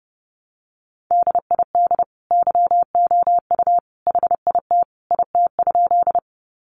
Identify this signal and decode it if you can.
Morse code, heard as a keyed tone, about 30 wpm, 700 Hz. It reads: DID YOU HIT IT?
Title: Sentences from Top 1200 Words 30wpm